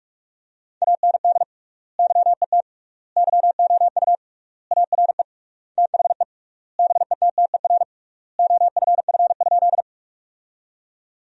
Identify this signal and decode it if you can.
Morse code, heard as a keyed tone, about 41 wpm, 700 Hz. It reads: AND YET YOU ARE THE BETTER OFF?
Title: 3x Word-Spacing - Sentences from Top 500 Words - 45wpm